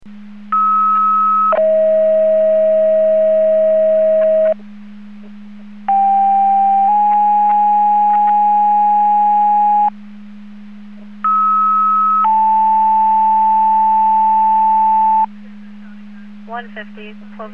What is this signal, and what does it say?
Two-tone page, heard as a one-second tone, then a three-second tone, about 1290 then 640 Hz, about 790 then 830 Hz, about 1290 then 880 Hz, with speech. It is heard from 0.5 to 15.3 s.